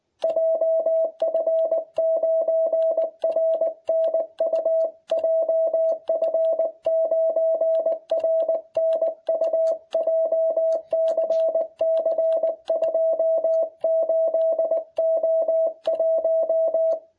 Morse code, heard as a keyed tone, about 18 wpm, 660 Hz. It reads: JF9RNU JF9RNU JCC 28O1